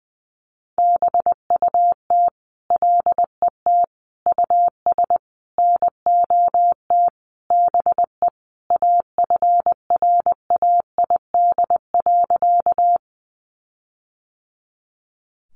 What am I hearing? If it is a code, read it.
Morse code, heard as a keyed tone, about 20 wpm, 700 Hz. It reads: BUT LET US NOT BE AFRAID.
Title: Sentences from Top 1500 Words 20wpm